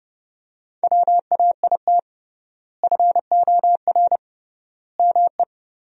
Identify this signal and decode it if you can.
Morse code, heard as a keyed tone, about 30 wpm, 700 Hz. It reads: WAIT FOR ME
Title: 3x Word-Spacing - Sentences from Top 500 Words - 30wpm